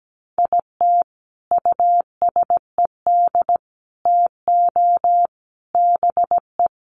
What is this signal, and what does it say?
Morse code, heard as a keyed tone, about 17 wpm, 700 Hz. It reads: IT USED TO BE
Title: Sets of 5 Words Encore2 17wpm